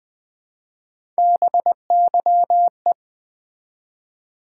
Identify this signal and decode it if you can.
Morse code, heard as a keyed tone, about 20 wpm, 700 Hz. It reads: BYE
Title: Build A Word - 1 Letter At A Time - 3-Letter Words - 20wpm